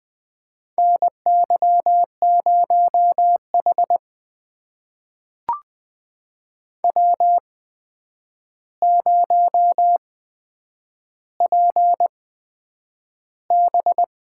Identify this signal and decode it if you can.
Morse code, heard as a keyed tone, about 20 wpm, spaced as for 9 wpm, 700 Hz. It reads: NY0H W0PB
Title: USA Call Signs - 4 Characters - 8x Character Spacing - 20wpm